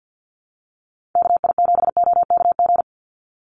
Morse code, heard as a keyed tone, about 48 wpm, 700 Hz. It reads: KI7OKG
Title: USA Call Signs 50wpm